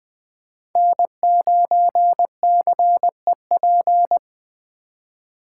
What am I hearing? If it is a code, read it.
Morse code, heard as a keyed tone, about 20 wpm, 700 Hz. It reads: N9CEP
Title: Speed Racing - USA 1x3 Call Signs 17wpm